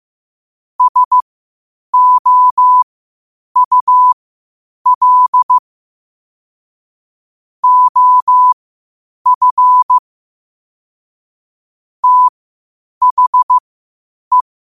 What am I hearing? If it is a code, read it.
Morse code, heard as a keyed tone, about 15 wpm, spaced as for 8 wpm, 1000 Hz. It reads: SOUL OF THE
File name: qotdfiles / 8wpm